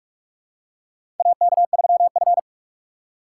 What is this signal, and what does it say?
Morse code, heard as a keyed tone, about 46 wpm, 700 Hz. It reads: AK3F
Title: USA Call Signs - 4 Characters - 4x Character Spacing - 45wpm